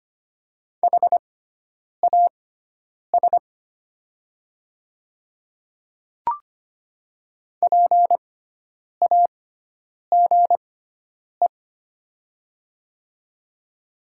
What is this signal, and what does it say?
Morse code, heard as a keyed tone, about 25 wpm, spaced as for 9 wpm, 700 Hz. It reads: HAS PAGE
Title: 6x Character Spacing - Top 300 Words - 25wpm